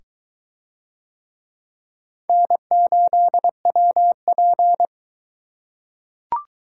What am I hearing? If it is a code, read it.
Morse code, heard as a keyed tone, about 22 wpm, 700 Hz. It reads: N8WP